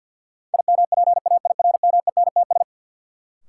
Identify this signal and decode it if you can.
Morse code, heard as a keyed tone, about 52 wpm, 700 Hz. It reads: IMPAIRMENTS
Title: Sets of 5 Words Encore - New Word Drill - 50wpm